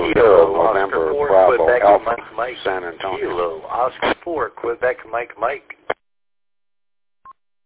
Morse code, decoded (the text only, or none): none